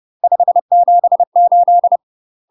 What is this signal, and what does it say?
Morse code, heard as a keyed tone, about 30 wpm, 700 Hz. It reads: 578
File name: UK MASTER QSOs TRAINING / 30WPM / UK MASTER QSOs 0049